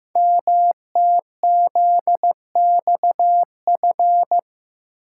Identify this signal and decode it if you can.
Morse code, heard as a keyed tone, about 15 wpm, 700 Hz. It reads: MTZXF
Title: Code Groups - 5 Letters - 15wpm